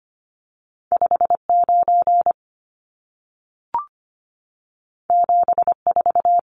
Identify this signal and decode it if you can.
Morse code, heard as a keyed tone, about 25 wpm, 700 Hz. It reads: 59 74
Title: Numbers - 2-Digit 25wpm